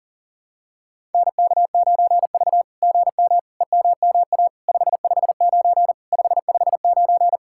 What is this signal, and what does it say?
Morse code, heard as a keyed tone, about 44 wpm, 700 Hz. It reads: NK9V GM EMMA 559 559
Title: SOTA QSO Practice 40wpm